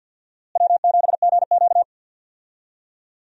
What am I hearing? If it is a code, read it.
Morse code, heard as a keyed tone, about 51 wpm, 700 Hz. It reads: W7GQ
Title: USA Call Signs - 4 Characters - Repeated 3x - 50wpm